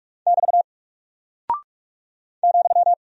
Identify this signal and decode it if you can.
Morse code, heard as a keyed tone, about 44 wpm, 700 Hz. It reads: <BT> ,
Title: Single Character (Letter:Number:Punctuation:Prosign) Alt 45wpm